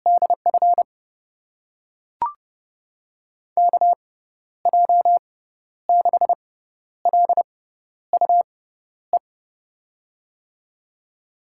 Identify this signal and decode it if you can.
Morse code, heard as a keyed tone, about 30 wpm, spaced as for 12 wpm, 700 Hz. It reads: DF KJ6LUE
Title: USA Call Signs - 6 Characters - 6x Character Spacing - 30wpm